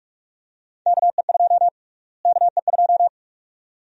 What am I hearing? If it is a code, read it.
Morse code, heard as a keyed tone, about 44 wpm, 700 Hz. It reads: KE2 KE2
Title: USA Call Sign Prefixes - Repeated 2x - 45wpm